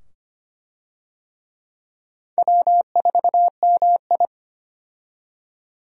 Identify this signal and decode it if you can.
Morse code, heard as a keyed tone, about 25 wpm, 700 Hz. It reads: W4MI
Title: USA Call Signs - 4 Characters - 25wpm